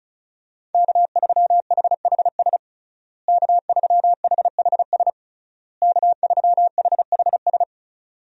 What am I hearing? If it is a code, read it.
Morse code, heard as a keyed tone, about 34 wpm, 700 Hz. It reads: K3HHS K3HHS K3HHS
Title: USA Call Signs - 5 Characters - Repeated 3x - 35wpm